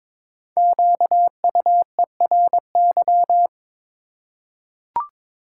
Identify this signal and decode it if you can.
Morse code, heard as a keyed tone, about 22 wpm, 700 Hz. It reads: QUERY
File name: 5 Letter Words - First 3 Letters Given - 22wpm